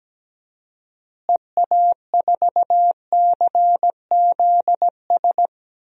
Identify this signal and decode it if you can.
Morse code, heard as a keyed tone, about 17 wpm, 700 Hz. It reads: EA4CZS
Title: DXCC-100 Call Signs - 6 Characters - Repeated 3x - 17wpm